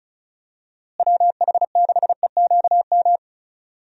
Morse code, heard as a keyed tone, about 36 wpm, 700 Hz. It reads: WH6EQM
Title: USA Call Signs - 6 Characters - Repeated 2x - 35wpm